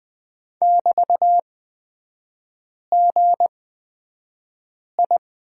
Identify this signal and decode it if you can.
Morse code, heard as a keyed tone, about 20 wpm, spaced as for 5 wpm, 700 Hz. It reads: <BT>GI
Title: 224 Lesson 36e - BT - Three Characters 20wpm